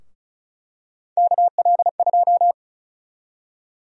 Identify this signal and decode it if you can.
Morse code, heard as a keyed tone, about 36 wpm, 700 Hz. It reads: KL2
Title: USA Call Sign Prefixes - Repeated 2x - 35wpm